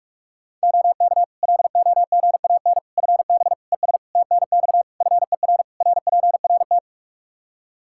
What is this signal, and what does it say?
Morse code, heard as a keyed tone, about 46 wpm, 700 Hz. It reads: OK LOGAN FB ES TNX FER RPRT
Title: Practice QSO - Exchange 2 - Info - 45wpm